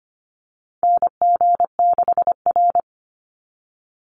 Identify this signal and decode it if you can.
Morse code, heard as a keyed tone, about 25 wpm, 700 Hz. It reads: NG6R
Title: NA SOTA Chaser Callsigns 25wpm